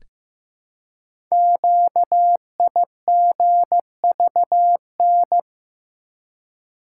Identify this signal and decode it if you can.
Morse code, heard as a keyed tone, about 15 wpm, 700 Hz. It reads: QIGVN
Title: Code Groups - 5 Letters - 15wpm